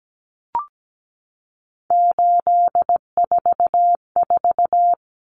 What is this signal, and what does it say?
Morse code, heard as a keyed tone, about 17 wpm, 700 Hz. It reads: 844
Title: Common Cut Numbers - 3-Digit 17wpm